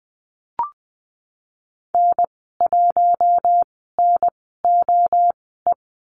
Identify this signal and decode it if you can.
Morse code, heard as a keyed tone, about 20 wpm, spaced as for 14 wpm, 700 Hz. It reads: N1NOE
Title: 042 Lesson 8g - Number 1 - Call Signs 20wpm